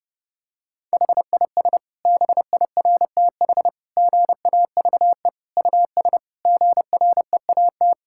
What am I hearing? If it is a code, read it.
Morse code, heard as a keyed tone, about 31 wpm, 700 Hz. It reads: HIS BIRTH GAVE US GREAT
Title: Sentences from Top 1000 Words - Review Part 5 - 30wpm